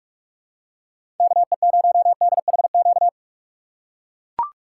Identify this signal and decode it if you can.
Morse code, heard as a keyed tone, about 44 wpm, 700 Hz. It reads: KE0DHQ
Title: USA Call Signs - 6x Character Spacing - 45wpm